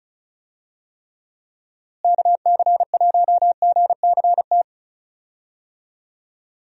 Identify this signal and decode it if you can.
Morse code, heard as a keyed tone, about 34 wpm, 700 Hz. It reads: KC1GCT
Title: USA Call Signs - 6 Characters - Repeated 3x - 35wpm